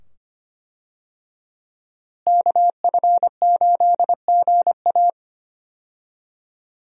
Morse code, heard as a keyed tone, about 25 wpm, 700 Hz. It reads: KF8GA
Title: USA Call Signs - 5 Characters - Repeated 2x - 25wpm